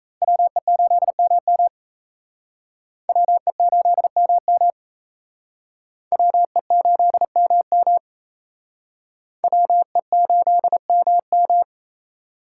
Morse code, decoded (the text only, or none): WE8MM WE8MM WE8MM WE8MM